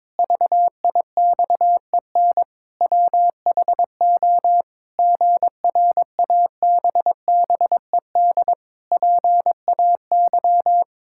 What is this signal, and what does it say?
Morse code, heard as a keyed tone, about 22 wpm, 700 Hz. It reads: VIXEN WHO GRABBED PAY
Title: Pangram Challenge v2 - 22wpm